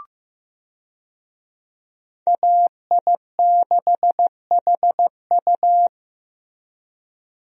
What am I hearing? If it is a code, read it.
Morse code, heard as a keyed tone, about 15 wpm, 700 Hz. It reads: AI6HU
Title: USA Call Signs - 5 Characters - Repeated 2x - 15wpm